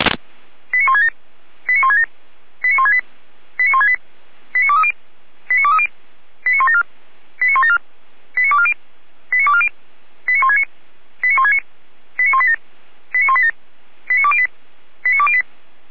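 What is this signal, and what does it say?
Five-tone selective call (ZVEI), heard as quick runs of five tones, tones of about 80 ms. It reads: 89167 89167 89168 89168 89230 89230 89174 89174 89240 89240 89169 89169 89187 89187 89208 89208
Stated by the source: ZVEI five-tone call received by SDR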